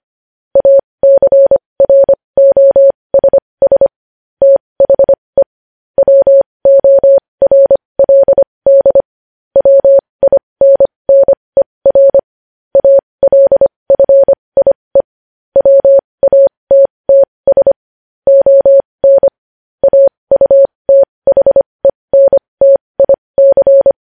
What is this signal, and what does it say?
Morse code, heard as a keyed tone, about 25 wpm, spaced as for 20 wpm, 550 Hz. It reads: ACROSS THE WORLD WINNER ALFIE WATTS ON AUTHENTIC
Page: news-headlines-in-morse-code-at-20-wpm